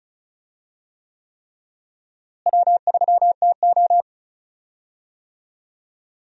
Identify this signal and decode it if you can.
Morse code, heard as a keyed tone, about 34 wpm, 700 Hz. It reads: W3TO
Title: USA Call Signs - 4 Characters - Repeated 3x - 35wpm